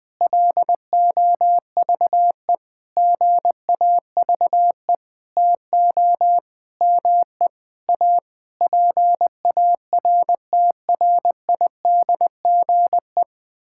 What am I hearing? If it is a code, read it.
Morse code, heard as a keyed tone, about 20 wpm, 700 Hz. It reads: LOVE GAVE TO ME A PARTRIDGE
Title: Sentences with 9-Word Phrases - 20wpm